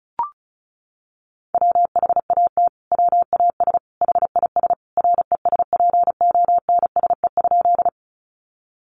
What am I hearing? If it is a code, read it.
Morse code, heard as a keyed tone, about 34 wpm, 700 Hz. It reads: WHAT WAS HIS RESPONSE?
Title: Sentences from Top 1400 Words 35wpm